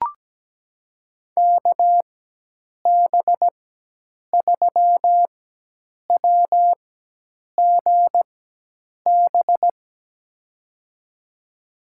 Morse code, heard as a keyed tone, about 17 wpm, spaced as for 8 wpm, 700 Hz. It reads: KB3WGB
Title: USA Call Signs - 6 Characters - 4x Character Spacing - 17wpm